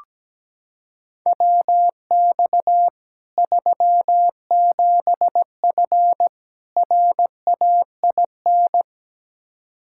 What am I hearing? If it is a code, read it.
Morse code, heard as a keyed tone, about 17 wpm, 700 Hz. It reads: WX 37F RAIN
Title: Practice QSO - Exchange 2 - Info - 17wpm